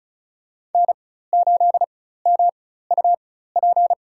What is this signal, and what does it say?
Morse code, heard as a keyed tone, about 34 wpm, spaced as for 16 wpm, 700 Hz. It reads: N8MUP
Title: USA Call Signs - 5 Characters - 4x Character Spacing - 35wpm